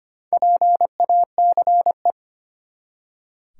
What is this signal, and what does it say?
Morse code, heard as a keyed tone, about 25 wpm, 700 Hz. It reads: PACE
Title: Four Letter Words - 25wpm